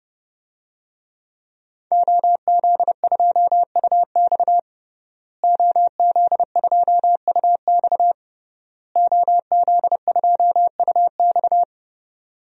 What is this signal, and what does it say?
Morse code, heard as a keyed tone, about 32 wpm, 700 Hz. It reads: OZ2UX OZ2UX OZ2UX
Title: DXCC-100 Call Signs - 5 Characters - Repeated 3x - 30wpm